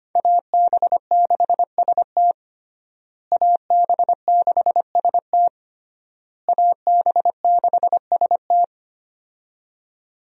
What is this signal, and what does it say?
Morse code, heard as a keyed tone, about 25 wpm, 700 Hz. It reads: AB6ST AB6ST AB6ST
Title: USA Call Signs - 5 Characters - Repeated 3x - 25wpm